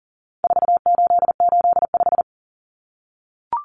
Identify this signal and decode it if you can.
Morse code, heard as a keyed tone, about 40 wpm, 700 Hz. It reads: 4885
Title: Common Cut Numbers - 4-Digit 40wpm